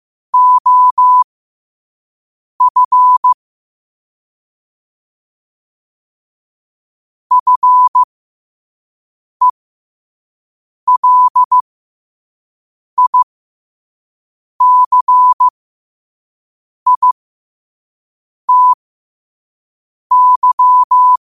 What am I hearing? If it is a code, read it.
Morse code, heard as a keyed tone, about 15 wpm, spaced as for 5 wpm, 1000 Hz. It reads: OF FELICITY
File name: qotdfiles / 5wpm